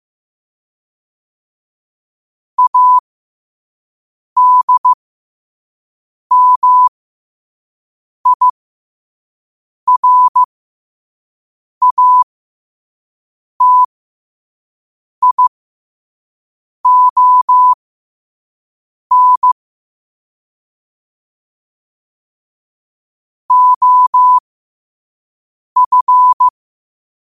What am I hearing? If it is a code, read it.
Morse code, heard as a keyed tone, about 15 wpm, spaced as for 5 wpm, 1000 Hz. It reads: ADMIRATION OF
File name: qotdfiles / 5wpm